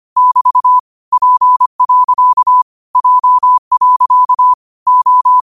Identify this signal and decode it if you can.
Morse code, heard as a keyed tone, about 25 wpm, 1000 Hz. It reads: <BT> P. J. O
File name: qotdfiles / 25wpm